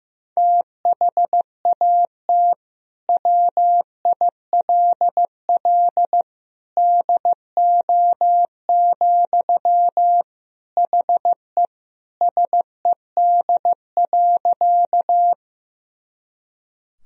Morse code, heard as a keyed tone, about 15 wpm, 700 Hz. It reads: THAT WILL DO, HE SED.
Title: Sentences from Top 100 Words and Abbreviations 15wpm